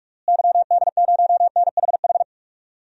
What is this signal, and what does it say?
Morse code, heard as a keyed tone, about 46 wpm, 700 Hz. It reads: YD0NHH
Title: DXCC-100 Call Signs - 6 Characters - Repeated 2x - 45wpm